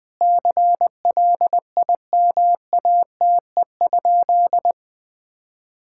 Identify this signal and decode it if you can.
Morse code, heard as a keyed tone, about 20 wpm, 700 Hz. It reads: CLIMATE?
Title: Sentences from Top 1600 Words 20wpm